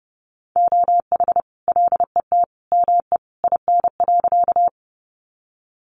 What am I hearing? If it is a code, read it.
Morse code, heard as a keyed tone, about 30 wpm, 700 Hz. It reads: OH LET ME IN.
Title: Sentences from Top 500 Words 30wpm